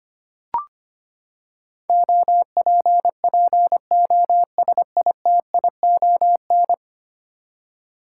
Morse code, heard as a keyed tone, about 25 wpm, 700 Hz. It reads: OPPOSITION
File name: Ten Letter Words - 25wpm